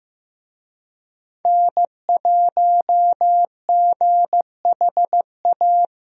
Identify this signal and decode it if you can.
Morse code, heard as a keyed tone, about 16 wpm, 700 Hz. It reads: N1GHA
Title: USA Call Signs - 5 Characters - Repeated 2x - 15wpm